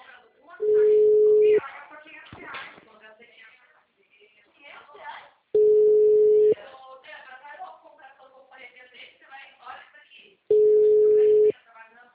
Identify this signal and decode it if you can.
Ringback tone (Europe, 425 Hz), with speech.